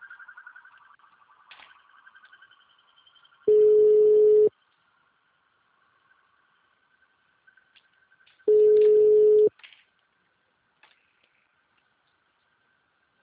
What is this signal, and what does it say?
Ringback tone (Europe, 425 Hz).
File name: out- -2003- - - .42388